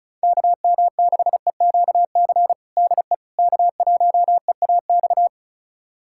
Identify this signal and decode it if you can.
Morse code, heard as a keyed tone, about 34 wpm, 700 Hz. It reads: KM6EQC DE K1EAX
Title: USA Callsign DE Callsign 35wpm